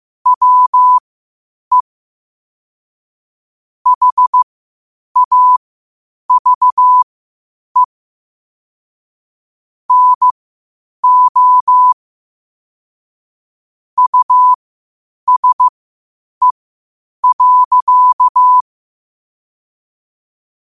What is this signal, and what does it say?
Morse code, heard as a keyed tone, about 15 wpm, spaced as for 8 wpm, 1000 Hz. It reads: WE HAVE NO USE.
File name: 8wpm